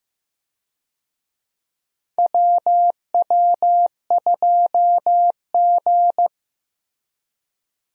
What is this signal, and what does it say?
Morse code, heard as a keyed tone, about 16 wpm, 700 Hz. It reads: WW2G
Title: USA Call Signs - 4 Characters - Repeated 2x - 15wpm